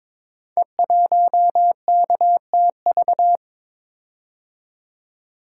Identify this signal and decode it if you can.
Morse code, heard as a keyed tone, about 22 wpm, 700 Hz. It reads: E1KTV
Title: Code Groups - 5 Letter:Numbers - 22wpm